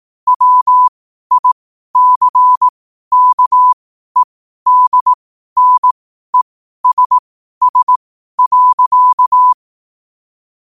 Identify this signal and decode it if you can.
Morse code, heard as a keyed tone, about 18 wpm, spaced as for 12 wpm, 1000 Hz. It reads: WICKEDNESS.
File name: qotdfiles / 12wpm